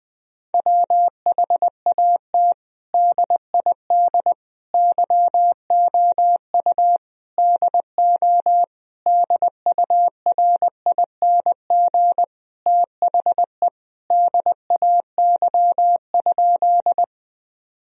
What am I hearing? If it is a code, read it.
Morse code, heard as a keyed tone, about 20 wpm, 700 Hz. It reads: WHAT DID YOU DO DURING THE DAY?
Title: Sentences from Top 1000 Words - Review Part 2 - 20wpm